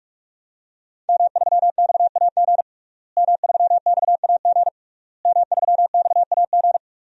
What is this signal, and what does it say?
Morse code, heard as a keyed tone, about 47 wpm, 700 Hz. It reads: M3XAG M3XAG M3XAG
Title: DXCC-100 Call Signs - 5 Characters - Repeated 3x - 45wpm